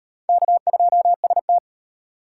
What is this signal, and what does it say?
Morse code, heard as a keyed tone, about 34 wpm, 700 Hz. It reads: K2ST